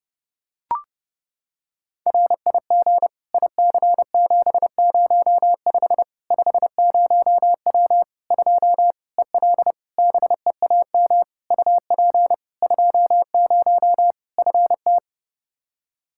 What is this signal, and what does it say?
Morse code, heard as a keyed tone, about 32 wpm, 700 Hz. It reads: RIG IC705 50W 2 EL BEAM UP 20 FT
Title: Practice QSO - Exchange 2 - Info - 30wpm